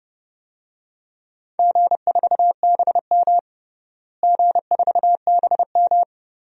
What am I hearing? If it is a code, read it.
Morse code, heard as a keyed tone, about 30 wpm, 700 Hz. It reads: G4BM G4BM
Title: DXCC-100 Call Signs - 4 Characters - Repeated 2x - 30wpm